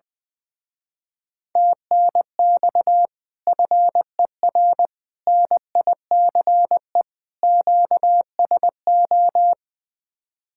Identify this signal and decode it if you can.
Morse code, heard as a keyed tone, about 20 wpm, 700 Hz. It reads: TNX FER NICE QSO
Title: Speed-Racing Common QSO Phrases - 20wpm